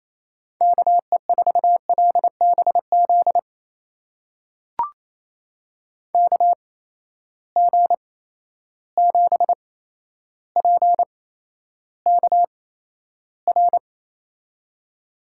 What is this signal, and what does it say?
Morse code, heard as a keyed tone, about 28 wpm, spaced as for 12 wpm, 700 Hz. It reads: KE4LBZ KG7PKR
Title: USA Call Signs - 6 Characters - 8x Character Spacing - 28wpm